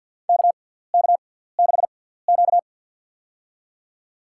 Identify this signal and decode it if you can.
Morse code, heard as a keyed tone, about 50 wpm, spaced as for 17 wpm, 690 Hz. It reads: KK6Q
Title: USA Call Signs - 4 Characters - 6x Character Spacing - 50wpm